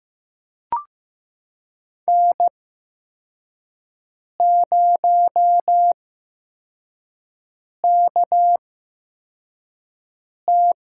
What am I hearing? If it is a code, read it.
Morse code, heard as a keyed tone, about 15 wpm, 700 Hz. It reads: N 0 K T
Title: USA Call Signs - 8x Character Spacing - 15wpm